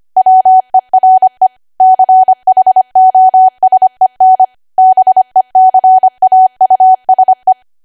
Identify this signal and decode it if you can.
Morse code, heard as a keyed tone, about 25 wpm, 750 Hz. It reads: WERE CHOSEN BECAUSE
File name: cwa / W1AW-Level3 / 25WPM